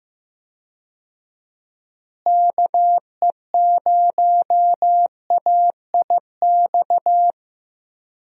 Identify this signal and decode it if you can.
Morse code, heard as a keyed tone, about 15 wpm, 700 Hz. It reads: KE0AIX
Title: USA Call Signs - 6 Characters - Repeated 2x - 15wpm